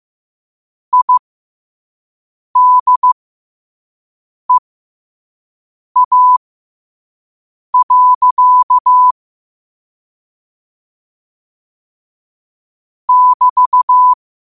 Morse code, heard as a keyed tone, about 15 wpm, spaced as for 5 wpm, 1000 Hz. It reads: IDEA. <BT>
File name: qotdfiles / 5wpm